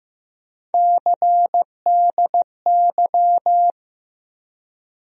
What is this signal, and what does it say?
Morse code, heard as a keyed tone, about 15 wpm, 700 Hz. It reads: CDY
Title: USA Call Sign Postfixes - 3 Letters 15wpm